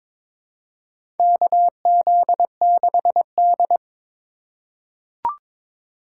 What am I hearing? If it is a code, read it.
Morse code, heard as a keyed tone, about 22 wpm, 700 Hz. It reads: KZ6D
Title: USA Call Signs - 4 Characters - 2x Character Spacing - 22wpm